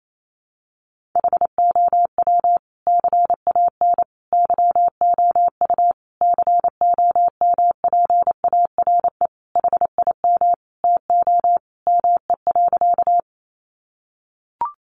Morse code, heard as a keyed tone, about 28 wpm, 700 Hz. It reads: HOW CAN YOU COMPARE HIM TO ME.
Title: Sentences from Top 1300 Words 28wpm